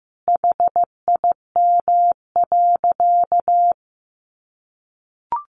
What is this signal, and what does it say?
Morse code, heard as a keyed tone, about 14 wpm, 700 Hz. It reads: HIM.
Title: Easy Phrases 15wpm